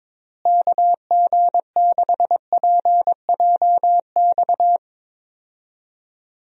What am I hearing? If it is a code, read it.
Morse code, heard as a keyed tone, about 22 wpm, 700 Hz. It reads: KG6PJX
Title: USA Call Signs - 6 Characters - Repeated 2x - 22wpm